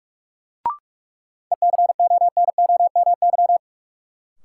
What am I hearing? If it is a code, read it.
Morse code, heard as a keyed tone, about 43 wpm, 700 Hz. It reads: ECONOMY
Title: Sentences from Top 1200 Words - New Word Drill Alt - 45wpm